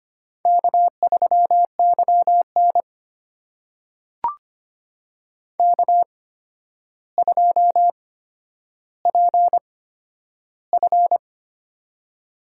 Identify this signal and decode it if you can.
Morse code, heard as a keyed tone, about 25 wpm, spaced as for 11 wpm, 700 Hz. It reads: K3YN K2PF
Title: USA Call Signs - 4 Characters - 8x Character Spacing - 25wpm